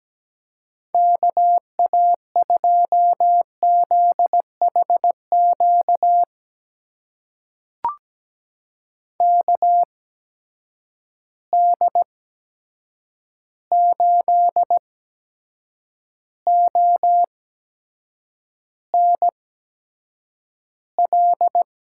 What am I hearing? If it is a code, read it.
Morse code, heard as a keyed tone, about 17 wpm, spaced as for 7 wpm, 700 Hz. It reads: KA2ZHQ KD8ONL
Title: USA Call Signs - 6 Characters - 8x Character Spacing - 17wpm